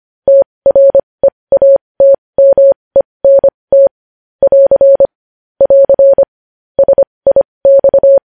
Morse code, heard as a keyed tone, about 25 wpm, spaced as for 20 wpm, 550 Hz. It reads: TREATMENT <AR> <AR> SIX
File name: podcast / News headlines-1x-25x20